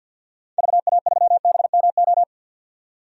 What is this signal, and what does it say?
Morse code, heard as a keyed tone, about 52 wpm, 700 Hz. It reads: VA3BMO
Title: DXCC-100 Call Signs - 6 Characters - 50wpm